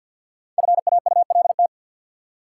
Morse code, heard as a keyed tone, about 47 wpm, 700 Hz. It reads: FAULT
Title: Five Letter Words - 50wpm